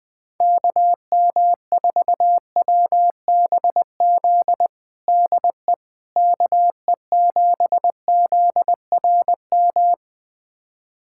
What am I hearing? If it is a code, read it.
Morse code, heard as a keyed tone, about 20 wpm, 700 Hz. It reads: KM4WBZ DE KE7ZRM
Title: USA Callsign DE Callsign 20wpm